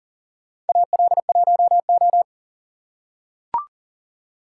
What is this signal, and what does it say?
Morse code, heard as a keyed tone, about 38 wpm, 700 Hz. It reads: AL1O
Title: USA Call Signs - 4 Characters - 6x Character Spacing - 40wpm